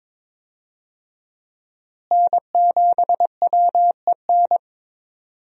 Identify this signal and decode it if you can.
Morse code, heard as a keyed tone, about 22 wpm, 700 Hz. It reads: N7WEN
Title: USA Call Signs - 5 Characters - Repeated 3x - 22wpm